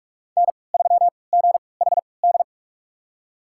Morse code, heard as a keyed tone, about 46 wpm, spaced as for 25 wpm, 700 Hz. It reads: N3GHD
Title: USA Call Signs - 5 Characters - 3x Character Spacing - 45wpm